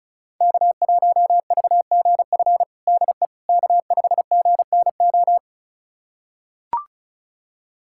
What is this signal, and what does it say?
Morse code, heard as a keyed tone, about 36 wpm, 700 Hz. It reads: K1VGF DE K5GNO
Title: USA Callsign DE Callsign - 5 Characters - 35wpm